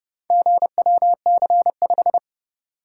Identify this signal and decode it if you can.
Morse code, heard as a keyed tone, about 29 wpm, 700 Hz. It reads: GWC5